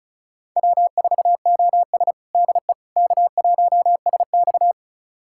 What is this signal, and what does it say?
Morse code, heard as a keyed tone, about 36 wpm, 700 Hz. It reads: W4OS DE K1SX
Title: USA Callsign DE Callsign - 4 Characters - 35wpm